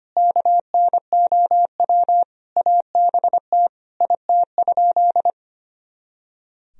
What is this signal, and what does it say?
Morse code, heard as a keyed tone, about 25 wpm, 700 Hz. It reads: KNOW ABT IT?
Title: Sentences from Top 100 Words and Abbreviations 25wpm